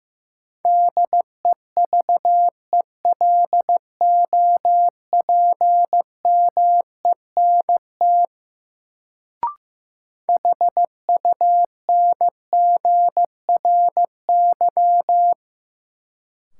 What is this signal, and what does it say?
Morse code, heard as a keyed tone, about 15 wpm, 700 Hz. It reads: DEVELOPMENT HUNGRY
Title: Sentences from Top 1400 Words - New Word Drill Alt - 15wpm